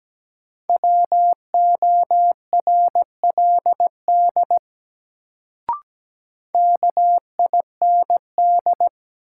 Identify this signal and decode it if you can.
Morse code, heard as a keyed tone, about 17 wpm, 700 Hz. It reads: WORLD KIND
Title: Sentences from Top 200 Words - New Word Drill - 17wpm